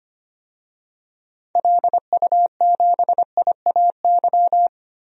Morse code, heard as a keyed tone, about 25 wpm, 700 Hz. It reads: LU7IAY
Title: DXCC-100 Call Signs - 6 Characters - 25wpm